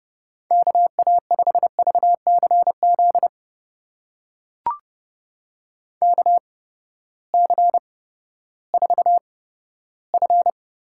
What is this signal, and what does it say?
Morse code, heard as a keyed tone, about 31 wpm, spaced as for 15 wpm, 700 Hz. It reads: KA5VCZ KC4F
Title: USA Call Signs - 6 Characters - 8x Character Spacing - 30wpm